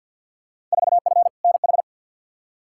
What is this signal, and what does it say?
Morse code, heard as a keyed tone, about 46 wpm, 700 Hz. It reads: 4F NH
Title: Field Day Exchanges Encore - 50wpm